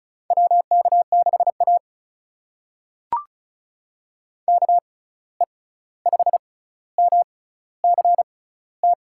Morse code, heard as a keyed tone, about 36 wpm, spaced as for 16 wpm, 700 Hz. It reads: WK6A KE5MCT
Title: USA Call Signs - 6x Character Spacing - 35wpm